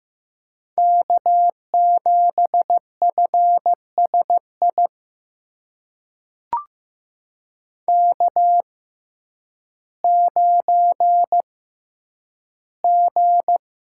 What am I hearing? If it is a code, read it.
Morse code, heard as a keyed tone, about 15 wpm, spaced as for 9 wpm, 700 Hz. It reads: K7FSI K9G
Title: USA Call Signs - 5 Characters - 6x Character Spacing - 15wpm